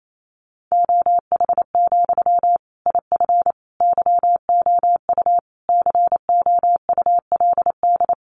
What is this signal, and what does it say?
Morse code, heard as a keyed tone, about 28 wpm, 700 Hz. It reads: OH, IF YOU COULD
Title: Sentences from Top 1100 Words 28wpm